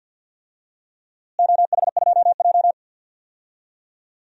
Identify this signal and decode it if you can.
Morse code, heard as a keyed tone, about 51 wpm, 700 Hz. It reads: OH2J
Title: DXCC-100 Call Signs - 4 Characters - Repeated 2x - 50wpm